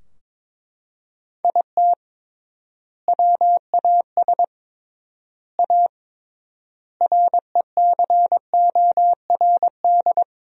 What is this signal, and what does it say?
Morse code, heard as a keyed tone, about 22 wpm, 700 Hz. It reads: IT WAS A RECORD
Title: 3x Word-Spacing - Sentences from Top 500 Words - 22wpm